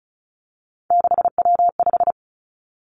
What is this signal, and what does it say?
Morse code, heard as a keyed tone, about 36 wpm, 700 Hz. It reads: 6W5
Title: Three Challenging Letter-Number Combinations 35wpm